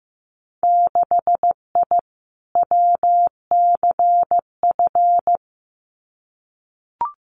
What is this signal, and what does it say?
Morse code, heard as a keyed tone, about 15 wpm, 700 Hz. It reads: 6I WCF
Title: Winter Field Day Exchanges Encore 15wpm